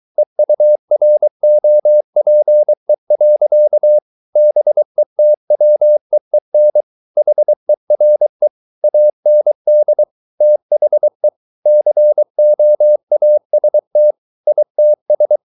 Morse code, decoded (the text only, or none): EUROPE. BETWEEN HERE AND THE COAST ITS